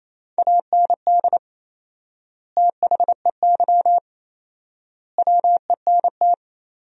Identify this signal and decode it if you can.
Morse code, heard as a keyed tone, about 28 wpm, 700 Hz. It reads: AND THEY WENT
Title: 4x Word-Spacing - Sentences from Top 200 Words - 28wpm